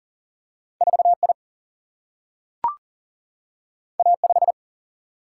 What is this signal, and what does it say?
Morse code, heard as a keyed tone, about 38 wpm, 700 Hz. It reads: 4I A5
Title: Single-Letter Single-Number 40wpm